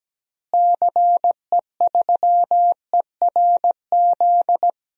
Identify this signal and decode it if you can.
Morse code, heard as a keyed tone, about 17 wpm, 700 Hz. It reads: CE3ERZ
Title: DXCC-100 Call Signs - 6 Characters - Repeated 3x - 17wpm